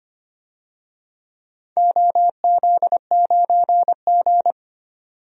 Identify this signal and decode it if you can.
Morse code, heard as a keyed tone, about 25 wpm, 700 Hz. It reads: OZ9G
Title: DXCC-100 Call Signs - 4 Characters - 25wpm